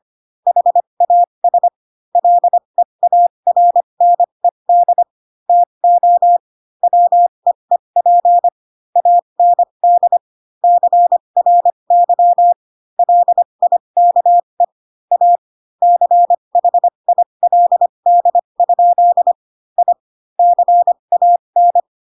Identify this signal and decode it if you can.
Morse code, heard as a keyed tone, about 25 wpm, spaced as for 22 wpm, 700 Hz. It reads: HAS LEARNED TO WEEP AND CRY LIKE A CHILD? I CAN